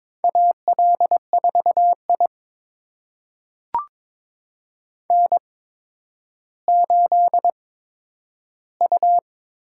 Morse code, heard as a keyed tone, about 22 wpm, spaced as for 11 wpm, 700 Hz. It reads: AL4I N8U
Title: USA Call Signs - 4 Characters - 8x Character Spacing - 22wpm